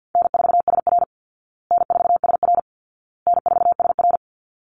Morse code, heard as a keyed tone, about 50 wpm, 700 Hz. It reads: N4SR N4SR N4SR
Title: Call Signs Repeated 2x - 50wpm